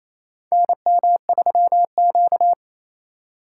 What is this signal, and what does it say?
Morse code, heard as a keyed tone, about 28 wpm, 700 Hz. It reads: NM3Q
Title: USA Call Signs - 2x Character Spacing - 28wpm